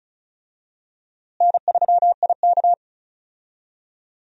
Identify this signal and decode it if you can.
Morse code, heard as a keyed tone, about 36 wpm, 700 Hz. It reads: N3IK
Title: USA Call Signs - 4 Characters - Repeated 3x - 35wpm